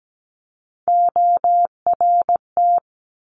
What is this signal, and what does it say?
Morse code, heard as a keyed tone, about 17 wpm, 700 Hz. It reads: ORT
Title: Top Three Letter Combinations in Words 17wpm